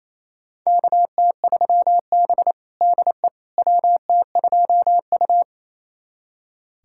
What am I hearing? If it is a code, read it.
Morse code, heard as a keyed tone, about 28 wpm, 700 Hz. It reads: KT3B DE WT2U
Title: USA Callsign DE Callsign - 4 Characters - 28wpm